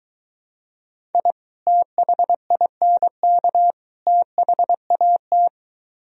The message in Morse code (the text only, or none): I THINK THAT